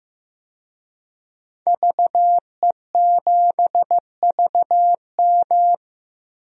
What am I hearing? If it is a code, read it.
Morse code, heard as a keyed tone, about 15 wpm, 700 Hz. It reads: VE7VM
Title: DXCC-100 Call Signs - 5 Characters - 15wpm